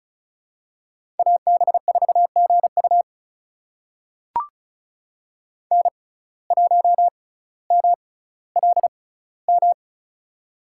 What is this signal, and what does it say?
Morse code, heard as a keyed tone, about 36 wpm, spaced as for 18 wpm, 700 Hz. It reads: AB4GU N1MLM
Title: USA Call Signs - 5 Characters - 6x Character Spacing - 35wpm